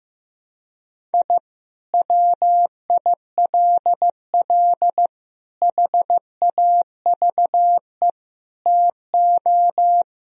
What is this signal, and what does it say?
Morse code, heard as a keyed tone, about 15 wpm, 700 Hz. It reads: I WILL HAVE TO
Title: Sentences from Top 1000 Words - Review Part 1 - 15wpm